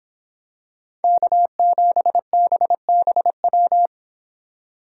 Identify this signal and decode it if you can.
Morse code, heard as a keyed tone, about 26 wpm, 700 Hz. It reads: K7BBW